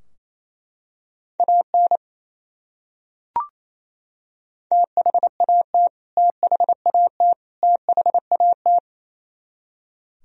Morse code, heard as a keyed tone, about 28 wpm, 700 Hz. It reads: AN THAT THAT THAT
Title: Top 100 Words Repeated 28wpm